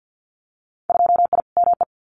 Morse code, heard as a keyed tone, about 50 wpm, 700 Hz. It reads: 2I ME
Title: Winter Field Day Exchanges Encore 50wpm